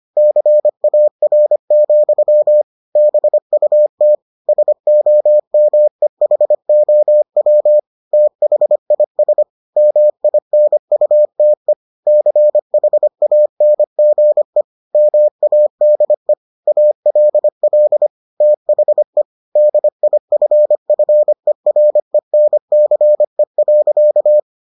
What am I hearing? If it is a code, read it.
Morse code, heard as a keyed tone, about 25 wpm, 600 Hz. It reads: CAR, BUT SOMEHOW THIS MINUTE CHANGE MADE ALL THE DIFFERENCE.